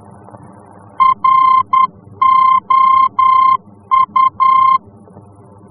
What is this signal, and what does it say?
Morse code, heard as a keyed tone, about 10 wpm, 1050 Hz. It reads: ROU
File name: france / mx / ROU-F-VOR